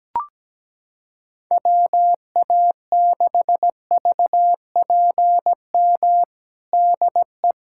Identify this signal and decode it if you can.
Morse code, heard as a keyed tone, about 17 wpm, 700 Hz. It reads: WA6VPM DE
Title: USA Callsign DE Callsign - 6 Characters - 17wpm